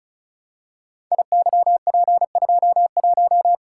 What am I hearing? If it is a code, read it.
Morse code, heard as a keyed tone, about 36 wpm, 700 Hz. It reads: IYP21
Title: Code Groups - 5 Letter:Numbers - 35wpm